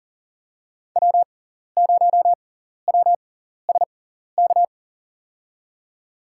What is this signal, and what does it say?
Morse code, heard as a keyed tone, about 40 wpm, spaced as for 14 wpm, 700 Hz. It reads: W0WSK
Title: USA Call Signs - 6x Character Spacing - 40wpm